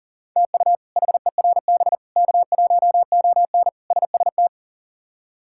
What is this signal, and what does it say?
Morse code, heard as a keyed tone, about 37 wpm, 690 Hz. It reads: TU HERB K1ON SST
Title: K1USN SST QSO Practice - 40wpm